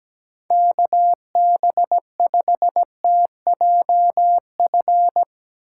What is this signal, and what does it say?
Morse code, heard as a keyed tone, about 18 wpm, 700 Hz. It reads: KB5TJF
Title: Speed Racing - USA 2x3 Call Signs 17wpm